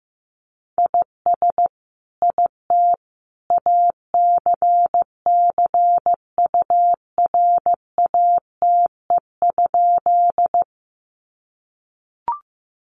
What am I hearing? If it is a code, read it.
Morse code, heard as a keyed tone, about 15 wpm, 700 Hz. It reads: IS IT ACCURATE?
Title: Sentences from Top 1200 Words 15wpm